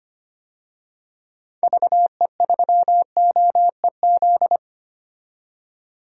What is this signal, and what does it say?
Morse code, heard as a keyed tone, about 25 wpm, 700 Hz. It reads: VE3OEZ
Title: DXCC-100 Call Signs - 6 Characters - 25wpm